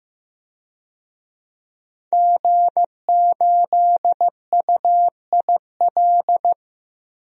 Morse code, heard as a keyed tone, about 15 wpm, 700 Hz. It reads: G8UIL
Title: DXCC-100 Call Signs - 5 Characters - Repeated 3x - 15wpm